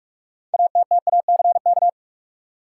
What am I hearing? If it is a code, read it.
Morse code, heard as a keyed tone, about 46 wpm, 700 Hz. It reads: ATTACK
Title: Sentences from Top 1500 Words - New Word Drill - 45wpm